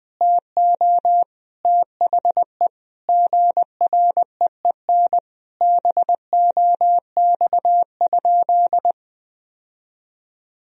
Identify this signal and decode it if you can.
Morse code, heard as a keyed tone, about 19 wpm, 700 Hz. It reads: TO THE GREEN BOX?
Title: Sentences from Top 1000 Words - Review Part 4 - 20wpm